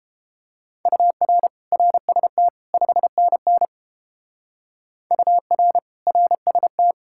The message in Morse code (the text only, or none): UR RST 5NN UR RST